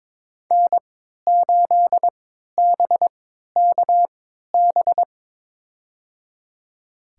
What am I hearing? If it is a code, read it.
Morse code, heard as a keyed tone, about 22 wpm, spaced as for 13 wpm, 700 Hz. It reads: N8BKB
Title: USA Call Signs - 5 Characters - 3x Character Spacing - 22wpm